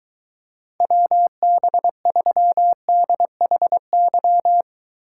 Morse code, heard as a keyed tone, about 23 wpm, 700 Hz. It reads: WB3DHY